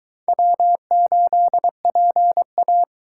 Morse code, heard as a keyed tone, about 22 wpm, 700 Hz. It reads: W8PA